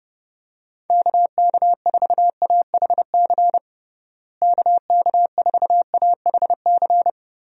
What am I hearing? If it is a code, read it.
Morse code, heard as a keyed tone, about 30 wpm, 700 Hz. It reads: KK4AHC KK4AHC
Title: USA Call Signs - 6 Characters - Repeated 3x - 30wpm